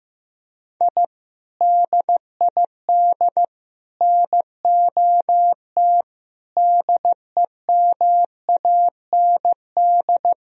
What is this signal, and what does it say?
Morse code, heard as a keyed tone, about 15 wpm, 700 Hz. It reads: I DID NOT DEMAND
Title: Sentences from Top 1000 Words - Review Part 4 - 15wpm